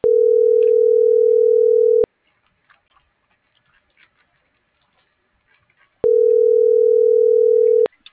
Ringback tone (North America), 440 + 480 Hz.